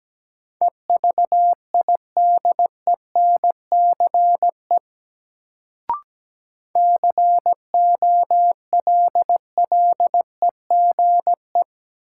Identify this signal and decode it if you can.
Morse code, heard as a keyed tone, about 17 wpm, 700 Hz. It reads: EVIDENCE COLLEGE
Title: Sentences from Top 1300 Words - New Word Drill - 17wpm